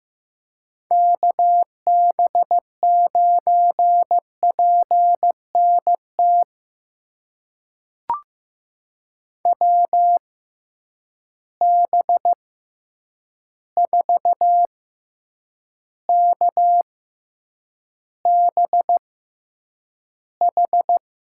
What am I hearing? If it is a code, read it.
Morse code, heard as a keyed tone, about 15 wpm, spaced as for 8 wpm, 700 Hz. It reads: KB9PNT WB4KBH
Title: USA Call Signs - 6 Characters - 6x Character Spacing - 15wpm